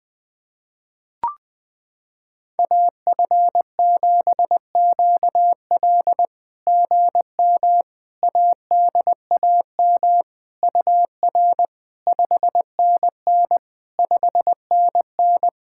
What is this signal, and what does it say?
Morse code, heard as a keyed tone, about 21 wpm, 700 Hz. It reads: AF7QL GM ADAM UR 5NN 5NN
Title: SOTA QSO Practice 20wpm